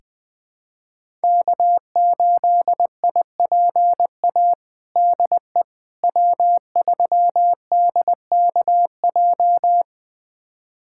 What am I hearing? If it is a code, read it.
Morse code, heard as a keyed tone, about 20 wpm, 700 Hz. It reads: K8IPA DE W3DKJ
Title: USA Callsign DE Callsign - 5 Characters - 20wpm